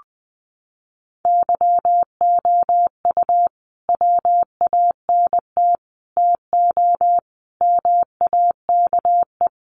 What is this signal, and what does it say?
Morse code, heard as a keyed tone, about 19 wpm, 700 Hz. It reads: YOU WANT TO MAKE
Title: Sentences from Top 600 Words 20wpm